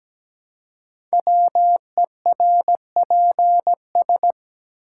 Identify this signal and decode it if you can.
Morse code, heard as a keyed tone, about 17 wpm, 700 Hz. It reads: WERPS
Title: Code Groups - 5 Letters - 17wpm